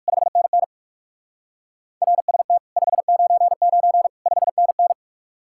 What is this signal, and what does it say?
Morse code, heard as a keyed tone, about 58 wpm, 700 Hz. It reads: 5NN RST 599 5NN